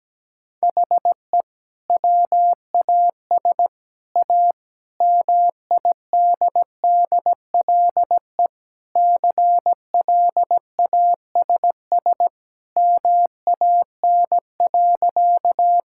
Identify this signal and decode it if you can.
Morse code, heard as a keyed tone, about 17 wpm, 700 Hz. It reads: HE WAS A MIDDLE CLASS MAN.
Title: Sentences from Top 1000 Words - Review Part 6 - 17wpm